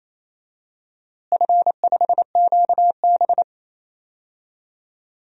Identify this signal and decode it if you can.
Morse code, heard as a keyed tone, about 28 wpm, 700 Hz. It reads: F5QB
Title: DXCC-100 Call Signs - 4 Characters - Repeated 3x - 28wpm